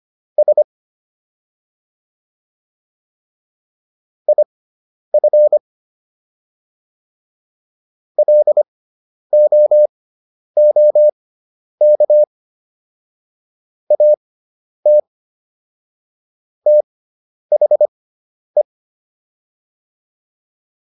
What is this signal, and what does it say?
Morse code, heard as a keyed tone, about 25 wpm, spaced as for 10 wpm, 600 Hz. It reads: S IF LOOK AT THE
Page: intermediate-practice-files